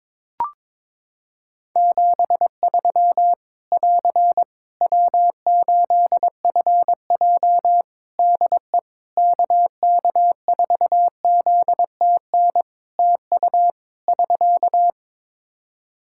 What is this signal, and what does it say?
Morse code, heard as a keyed tone, about 22 wpm, 700 Hz. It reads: 73 <AR> W8FJ DE KK4ZTN TU <SK>
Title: Practice QSO - Exchange 4 - Ending - 22wpm